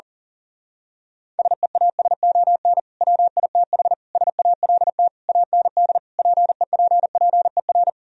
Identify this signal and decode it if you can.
Morse code, heard as a keyed tone, about 37 wpm, 700 Hz. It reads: SEASON WITH SALT AND PEPPER
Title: Speed-Racing - Sets of 5 Words 40wpm